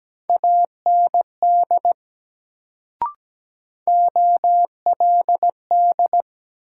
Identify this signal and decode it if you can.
Morse code, heard as a keyed tone, about 17 wpm, 700 Hz. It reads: AND OLD
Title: Sets of 2 Words - New Word Drill - 17wpm